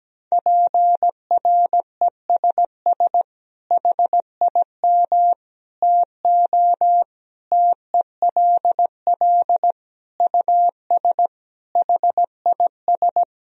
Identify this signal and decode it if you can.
Morse code, heard as a keyed tone, about 17 wpm, 700 Hz. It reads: PRESS HIM TO TELL US HIS
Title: Sentences from Top 1000 Words - Review Part 1 - 17wpm